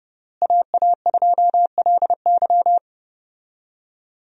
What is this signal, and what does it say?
Morse code, heard as a keyed tone, about 30 wpm, 700 Hz. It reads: AA2LY
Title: USA Call Signs - 5 Characters - Repeated 3x - 30wpm